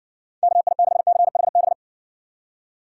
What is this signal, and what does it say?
Morse code, heard as a keyed tone, about 55 wpm, 700 Hz. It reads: KI6CHB